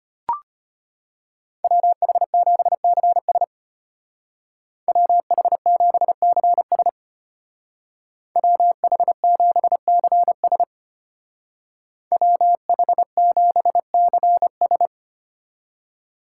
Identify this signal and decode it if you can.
Morse code, heard as a keyed tone, about 30 wpm, 700 Hz. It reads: WH7CS WH7CS WH7CS WH7CS
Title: Speed Racing - USA 2x2 Call Signs 25wpm